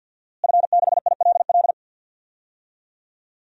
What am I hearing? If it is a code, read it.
Morse code, heard as a keyed tone, about 53 wpm, 700 Hz. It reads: F6ILL
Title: DXCC-100 Call Signs - 5 Characters - Repeated 2x - 50wpm